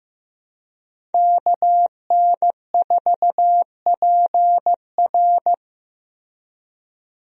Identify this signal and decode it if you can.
Morse code, heard as a keyed tone, about 15 wpm, 700 Hz. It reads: KN4PR
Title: USA Call Signs - 5 Characters - Repeated 2x - 15wpm